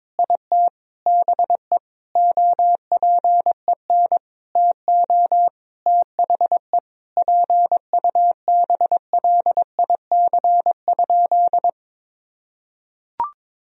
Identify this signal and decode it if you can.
Morse code, heard as a keyed tone, about 21 wpm, 700 Hz. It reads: IT BE OPEN TO THE PUBLIC?
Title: Sentences from Top 1000 Words - Review Part 3 - 22wpm